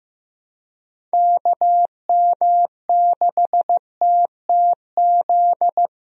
Morse code, heard as a keyed tone, about 15 wpm, 700 Hz. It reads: KM6TTZ
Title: USA Call Signs - 6 Characters - Repeated 2x - 15wpm